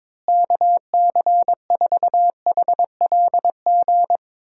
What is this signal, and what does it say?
Morse code, heard as a keyed tone, about 22 wpm, 700 Hz. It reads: KC4HLG
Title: USA Call Signs - 6 Characters - Repeated 3x - 22wpm